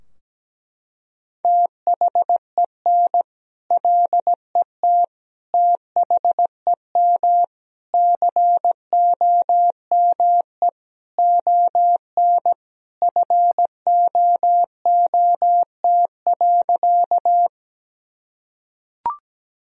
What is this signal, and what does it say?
Morse code, heard as a keyed tone, about 17 wpm, 700 Hz. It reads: THEN LET THEM COME ON FOOT.
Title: Sentences from Top 1000 Words - Review Part 3 - 17wpm